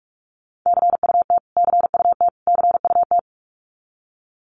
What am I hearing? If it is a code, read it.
Morse code, heard as a keyed tone, about 44 wpm, 700 Hz. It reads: CUT CUT CUT
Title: Top 500 Words Repeated 45wpm